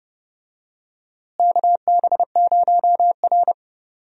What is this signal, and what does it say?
Morse code, heard as a keyed tone, about 31 wpm, 700 Hz. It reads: KB0R
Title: USA Call Signs - 4 Characters - Repeated 3x - 30wpm